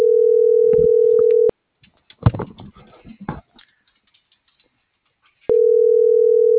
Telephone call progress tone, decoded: ringback tone

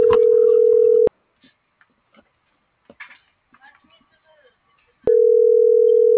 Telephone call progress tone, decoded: ringback tone

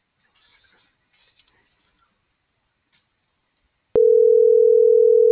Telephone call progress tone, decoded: ringback tone

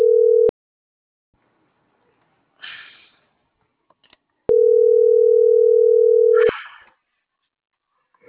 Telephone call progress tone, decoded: ringback tone

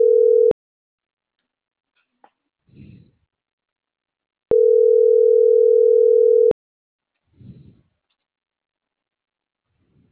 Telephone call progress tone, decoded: ringback tone

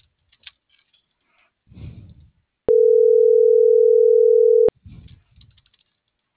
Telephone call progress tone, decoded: ringback tone

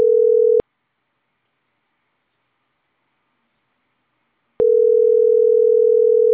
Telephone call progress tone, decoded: ringback tone

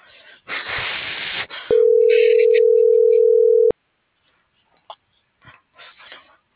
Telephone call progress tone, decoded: ringback tone

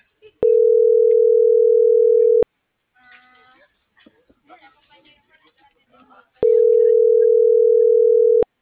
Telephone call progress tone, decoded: ringback tone

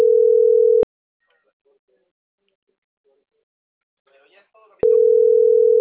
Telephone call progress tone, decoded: ringback tone

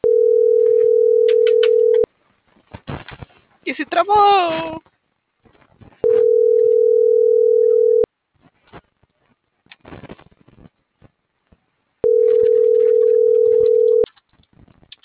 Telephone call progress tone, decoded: ringback tone

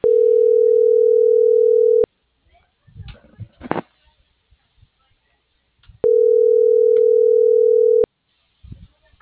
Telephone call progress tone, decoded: ringback tone

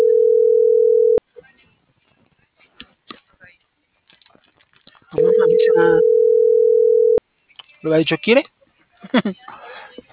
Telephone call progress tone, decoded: ringback tone

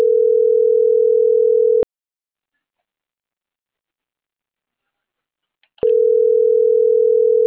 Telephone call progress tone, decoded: ringback tone

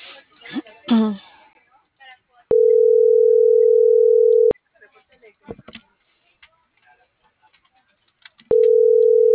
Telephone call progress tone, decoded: ringback tone

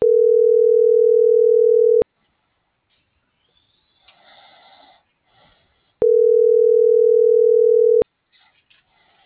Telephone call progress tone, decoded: ringback tone

